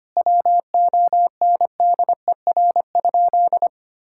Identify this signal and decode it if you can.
Morse code, heard as a keyed tone, about 25 wpm, 700 Hz. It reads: WONDER?
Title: Sentences from Top 1000 Words - Review Part 5 - 25wpm